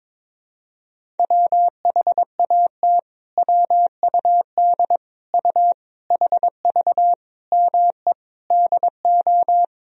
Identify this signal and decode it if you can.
Morse code, heard as a keyed tone, about 22 wpm, 700 Hz. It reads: WHAT WUD U HV ME DO
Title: Sentences from Top 300 Words and Abbreviations 22wpm